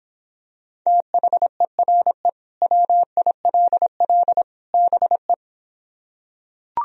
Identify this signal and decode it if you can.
Morse code, heard as a keyed tone, about 26 wpm, 700 Hz. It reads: THERE WILL BE